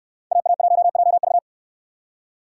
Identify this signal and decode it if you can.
Morse code, heard as a keyed tone, about 69 wpm, 700 Hz. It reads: AA1JV